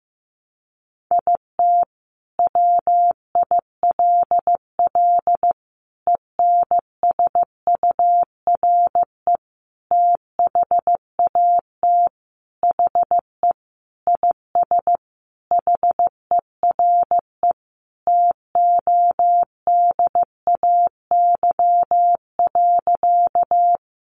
Morse code, heard as a keyed tone, about 15 wpm, 700 Hz. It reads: IT WILL ENSURE THAT HE IS HERE TODAY.
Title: Sentences from Top 1100 Words 15wpm